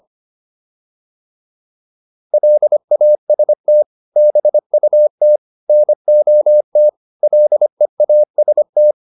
Morse code, heard as a keyed tone, about 25 wpm, 600 Hz. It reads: LAST BUT NOT LEAST
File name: PR403 25